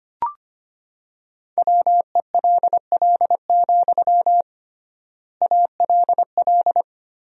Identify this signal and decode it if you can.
Morse code, heard as a keyed tone, about 25 wpm, 700 Hz. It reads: WELL, ALL
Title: 3x Word-Spacing - Sentences from Top 200 Words - 25wpm